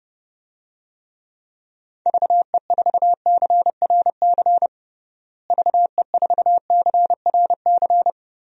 Morse code, heard as a keyed tone, about 30 wpm, 700 Hz. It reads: VE4CRC VE4CRC
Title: DXCC-100 Call Signs - 6 Characters - Repeated 2x - 30wpm